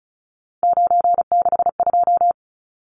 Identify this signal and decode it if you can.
Morse code, heard as a keyed tone, about 34 wpm, 700 Hz. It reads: 962
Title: Numbers - 3-Digit 35wpm